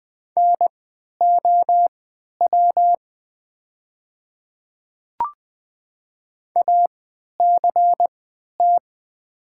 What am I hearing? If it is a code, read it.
Morse code, heard as a keyed tone, about 20 wpm, spaced as for 11 wpm, 700 Hz. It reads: NOW ACT
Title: 3x Character Spacing - Top 200 Words - 20wpm